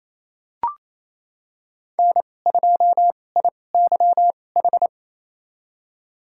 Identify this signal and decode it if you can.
Morse code, heard as a keyed tone, about 28 wpm, spaced as for 20 wpm, 700 Hz. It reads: N2IYH
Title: USA Call Signs - 5 Characters - 2x Character Spacing - 28wpm